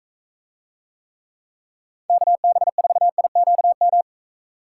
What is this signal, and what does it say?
Morse code, heard as a keyed tone, about 41 wpm, 700 Hz. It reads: KB4IQM